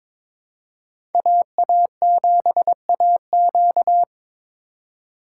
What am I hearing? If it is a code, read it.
Morse code, heard as a keyed tone, about 25 wpm, 700 Hz. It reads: AA7AQ
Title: Speed Racing - USA 2x2 Call Signs 22wpm